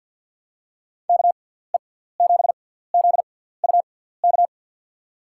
Morse code, heard as a keyed tone, about 53 wpm, spaced as for 17 wpm, 700 Hz. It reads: KE7ZUK
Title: USA Call Signs - 6 Characters - 6x Character Spacing - 50wpm